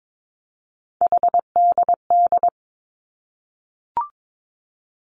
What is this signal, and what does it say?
Morse code, heard as a keyed tone, about 22 wpm, 700 Hz. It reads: HDD
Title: Three Challenging Letter-Number Combinations 22wpm